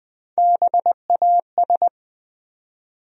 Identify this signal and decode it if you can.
Morse code, heard as a keyed tone, about 20 wpm, 700 Hz. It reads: BAS
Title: USA Call Sign Postfixes - 3 Letters 20wpm